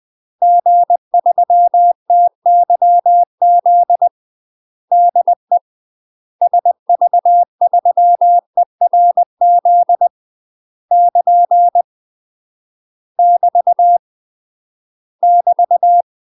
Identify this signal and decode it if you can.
Morse code, heard as a keyed tone, about 20 wpm, 700 Hz. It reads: G3TYZ DE SV3ERZ <KN> <BT> <BT>
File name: UK MASTER QSOs TRAINING / 20WPM / UK MASTER QSOs 0045